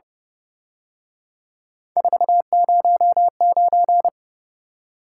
Morse code, heard as a keyed tone, about 30 wpm, 700 Hz. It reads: 409